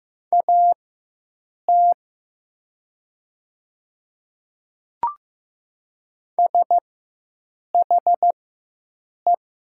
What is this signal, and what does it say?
Morse code, heard as a keyed tone, about 15 wpm, spaced as for 7 wpm, 700 Hz. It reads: AT SHE
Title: 4x Character Spacing - Top 100 Words - 15wpm